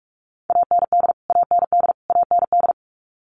Morse code, heard as a keyed tone, about 44 wpm, 700 Hz. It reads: AND AND AND
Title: Top 200 Words Repeated 45wpm